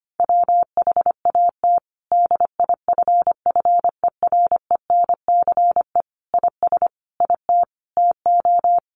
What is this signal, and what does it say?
Morse code, heard as a keyed tone, about 25 wpm, 700 Hz. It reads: WHAT DIFFERENCE IS IT TO
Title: Sentences from Top 800 Words 25wpm